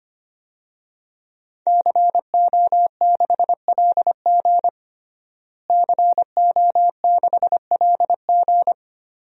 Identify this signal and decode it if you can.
Morse code, heard as a keyed tone, about 25 wpm, 700 Hz. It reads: CO6LG CO6LG
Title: DXCC-100 Call Signs - 5 Characters - Repeated 3x - 25wpm